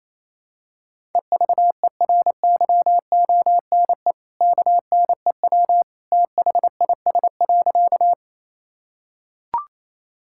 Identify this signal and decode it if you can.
Morse code, heard as a keyed tone, about 28 wpm, 700 Hz. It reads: EVERYONE KNEW THIS.
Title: Sentences from Top 1000 Words - Review Part 5 - 28wpm